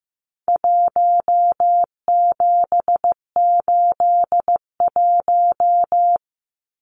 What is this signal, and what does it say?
Morse code, heard as a keyed tone, about 15 wpm, 700 Hz. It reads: 1781
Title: Numbers - Contest Style - 4-Digit 15wpm